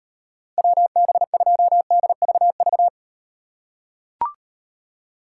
Morse code, heard as a keyed tone, about 38 wpm, 700 Hz. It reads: WB2DVV